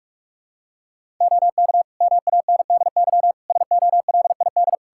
Morse code, heard as a keyed tone, about 46 wpm, 700 Hz. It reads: OK MANDY SOLID
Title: Practice QSO - Exchange 3 - More Info - 45wpm